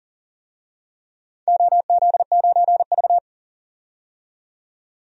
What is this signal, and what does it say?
Morse code, heard as a keyed tone, about 40 wpm, 700 Hz. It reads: OZ9V
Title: DXCC-100 Call Signs - 4 Characters - 40wpm